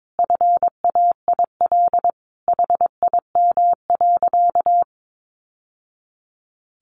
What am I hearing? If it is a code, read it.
Morse code, heard as a keyed tone, about 22 wpm, 700 Hz. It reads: FAIL HIM.
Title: Sentences from Top 1100 Words 22wpm